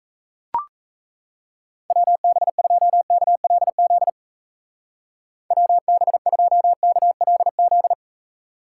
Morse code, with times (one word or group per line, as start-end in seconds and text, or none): 1.9-4.1 WB2KLZ
5.5-7.9 WB2KLZ